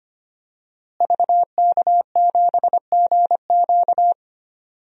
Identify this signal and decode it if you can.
Morse code, heard as a keyed tone, about 25 wpm, 700 Hz. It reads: VK7GQ
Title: DXCC-100 Call Signs - 5 Characters - Repeated 2x - 25wpm